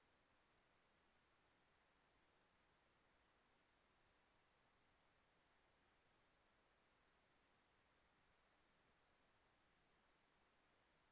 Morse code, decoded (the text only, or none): none